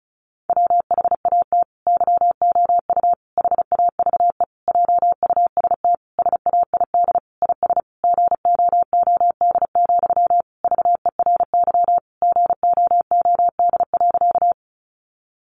Morse code, heard as a keyed tone, about 34 wpm, 700 Hz. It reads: WHAT YOU HAVE JUST SAID IS GOOD, VERY GOOD.
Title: Sentences from Top 200 Words 35wpm